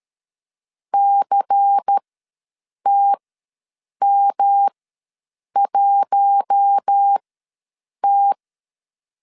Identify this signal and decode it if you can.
Morse code, heard as a keyed tone, about 13 wpm, spaced as for 7 wpm, 800 Hz. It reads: CTM1T
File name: LEZIONI / 7 GF9Z / 7 GF ws 12wpm 3140